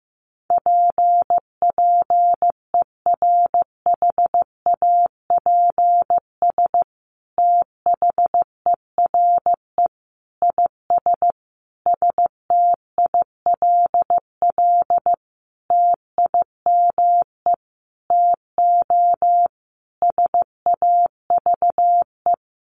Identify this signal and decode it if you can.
Morse code, heard as a keyed tone, about 15 wpm, 700 Hz. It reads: P PERHAPS THERE IS STILL TIME TO SAVE
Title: Sentences from Top 1100 Words 15wpm